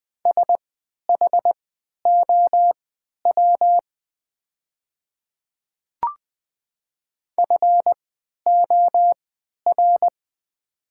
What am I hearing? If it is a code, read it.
Morse code, heard as a keyed tone, about 20 wpm, spaced as for 11 wpm, 700 Hz. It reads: SHOW FOR
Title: 3x Character Spacing - Top 200 Words - 20wpm